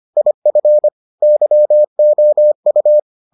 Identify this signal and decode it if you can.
Morse code, heard as a keyed tone, about 25 wpm, 600 Hz. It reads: IF YOU